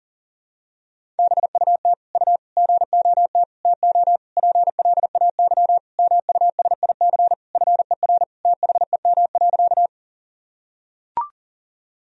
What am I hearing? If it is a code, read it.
Morse code, heard as a keyed tone, about 40 wpm, 700 Hz. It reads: BUT U GOT TO PLAY MUSIC FER THEM.
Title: Sentences from Top 300 Words and Abbreviations 40wpm